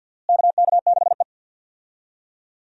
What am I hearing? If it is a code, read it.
Morse code, heard as a keyed tone, about 47 wpm, 700 Hz. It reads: KK6E
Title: USA Call Signs - 4 Characters - Repeated 3x - 50wpm